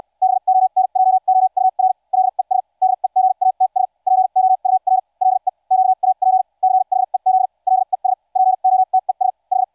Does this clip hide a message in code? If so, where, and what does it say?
Yes: Morse code from 0.2 to 9.6 s, heard as a keyed tone, about 74 wpm, spaced as for 52 wpm, 750 Hz. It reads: BONJOUR LES DEGATS POUR LE QAP CHEZ LES POTES H